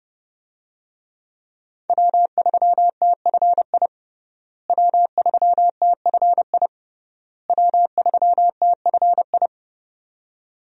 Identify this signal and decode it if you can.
Morse code, heard as a keyed tone, about 32 wpm, 700 Hz. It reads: W3TFI W3TFI W3TFI
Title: USA Call Signs - 5 Characters - Repeated 3x - 30wpm